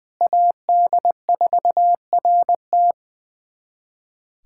Morse code, heard as a keyed tone, about 21 wpm, 700 Hz. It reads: AD4RT
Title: USA Callsign DE Callsign 20wpm